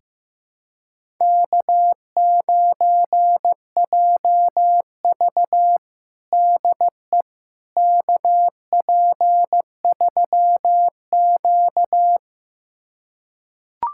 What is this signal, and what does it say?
Morse code, heard as a keyed tone, about 15 wpm, 700 Hz. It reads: K9JV DE KP3Q
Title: USA Callsign DE Callsign - 4 Characters - 15wpm